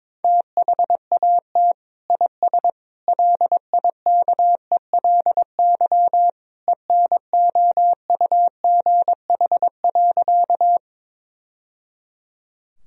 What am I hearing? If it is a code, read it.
Morse code, heard as a keyed tone, about 22 wpm, 700 Hz. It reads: THAT IS LIKELY ENOUGH.
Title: Sentences from Top 1000 Words - Review Part 4 - 22wpm